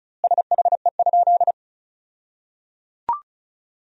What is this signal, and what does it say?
Morse code, heard as a keyed tone, about 36 wpm, 700 Hz. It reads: SHE?
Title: 2x Word-Spacing - Sentences from Top 100 Words - 35wpm